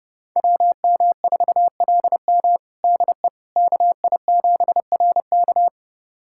Morse code, heard as a keyed tone, about 32 wpm, 700 Hz. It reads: WM4LM DE KI7RK
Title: USA Callsign DE Callsign - 5 Characters - 30wpm